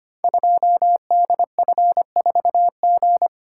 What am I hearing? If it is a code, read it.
Morse code, heard as a keyed tone, about 25 wpm, 700 Hz. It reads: 2DF4G
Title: Code Groups - 5 Letter:Numbers - 25wpm